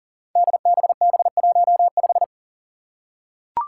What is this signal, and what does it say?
Morse code, heard as a keyed tone, about 39 wpm, 700 Hz. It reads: DBB15
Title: Five Challenging Letter-Number Combinations 40wpm